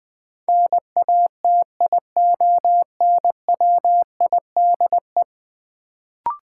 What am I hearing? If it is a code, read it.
Morse code, heard as a keyed tone, about 20 wpm, 700 Hz. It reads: NATIONWIDE
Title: Build A Word - 5 Letters At A Time - 10-Letter Words - 20wpm